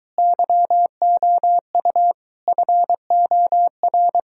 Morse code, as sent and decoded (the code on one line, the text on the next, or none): -.-- --- ..- / ..-. --- .-.
YOU FOR